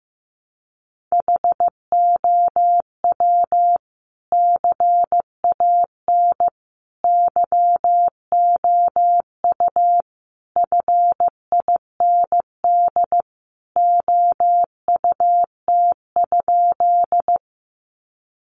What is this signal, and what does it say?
Morse code, heard as a keyed tone, about 14 wpm, 700 Hz. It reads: HOW CAN YOU FIND OUT?
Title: Sentences from Top 200 Words 15wpm